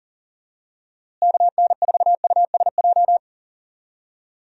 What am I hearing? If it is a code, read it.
Morse code, heard as a keyed tone, about 39 wpm, 700 Hz. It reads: KN4USJ
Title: USA Call Signs - 6 Characters - Repeated 3x - 40wpm